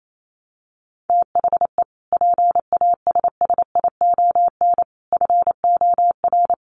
Morse code, heard as a keyed tone, about 28 wpm, 700 Hz. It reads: THE PASSION FOR
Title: Sentences from Top 1300 Words 28wpm